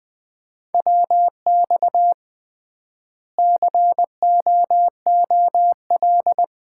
Morse code, heard as a keyed tone, about 20 wpm, 700 Hz. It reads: WX COOL
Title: Common QSO Phrases - 3x Word-Spacing - 20wpm